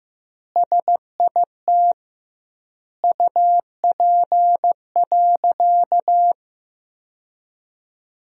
Morse code, heard as a keyed tone, about 16 wpm, 700 Hz. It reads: SIT UP.
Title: 2x Word-Spacing - Sentences from Top 500 Words - 15wpm